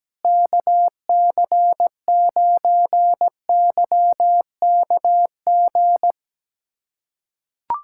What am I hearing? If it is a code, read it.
Morse code, heard as a keyed tone, about 17 wpm, 700 Hz. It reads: KC9YKG
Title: USA Callsign DE Callsign - 6 Characters - 17wpm